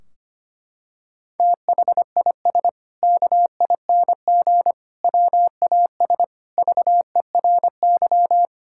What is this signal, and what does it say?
Morse code, heard as a keyed tone, about 25 wpm, 700 Hz. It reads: THIS KING WAS VERY
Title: Sentences from Top 1600 Words 25wpm